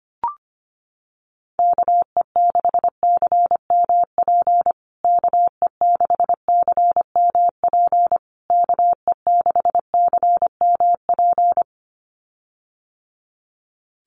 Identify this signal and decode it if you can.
Morse code, heard as a keyed tone, about 25 wpm, 700 Hz. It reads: KE6CMP KE6CMP KE6CMP
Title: USA Call Signs Repeated 3x 25wpm